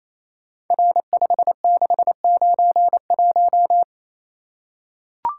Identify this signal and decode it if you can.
Morse code, heard as a keyed tone, about 27 wpm, 700 Hz. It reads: R5691
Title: Code Groups - 5 Letter:Numbers - 28wpm